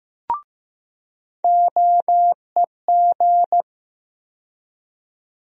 Morse code, heard as a keyed tone, about 15 wpm, 700 Hz. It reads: OEG
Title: USA Call Sign Postfixes - 3 Letters 15wpm